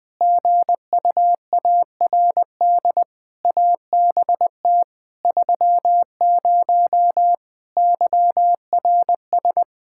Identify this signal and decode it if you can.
Morse code, heard as a keyed tone, about 21 wpm, 700 Hz. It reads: GUARD ABT 30 YRS
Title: K1USN SST QSO Practice 20wpm